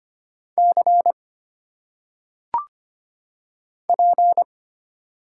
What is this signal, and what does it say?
Morse code, heard as a keyed tone, about 25 wpm, 700 Hz. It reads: C P
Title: Single Letters - Phonetic Pronunciation 25wpm